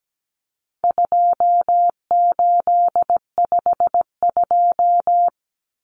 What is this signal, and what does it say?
Morse code, heard as a keyed tone, about 17 wpm, 700 Hz. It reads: 2852
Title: Common Cut Numbers - 4-Digit 17wpm